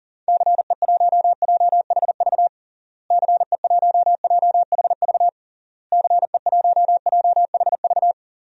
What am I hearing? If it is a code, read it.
Morse code, heard as a keyed tone, about 40 wpm, 700 Hz. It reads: CE1JHV CE1JHV CE1JHV
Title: DXCC-100 Call Signs - 6 Characters - Repeated 3x - 40wpm